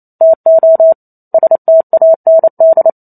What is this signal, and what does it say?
Morse code, heard as a keyed tone, about 28 wpm, 650 Hz. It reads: TO STAND
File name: Chapter 2 - 28 WPM